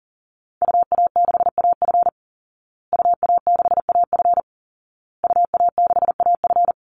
Morse code, heard as a keyed tone, about 40 wpm, 700 Hz. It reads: UA6AF UA6AF UA6AF
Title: Call Signs Repeated 2x - 40wpm